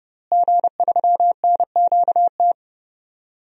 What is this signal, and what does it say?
Morse code, heard as a keyed tone, about 31 wpm, 700 Hz. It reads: G3NQT
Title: DXCC-100 Call Signs - 5 Characters - Repeated 2x - 30wpm